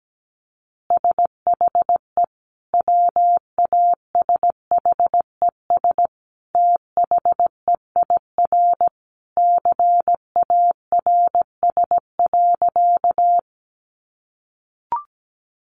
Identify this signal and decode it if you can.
Morse code, heard as a keyed tone, about 17 wpm, 700 Hz. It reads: SHE WASHES THEIR CARS.
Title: Easy Phrases 17wpm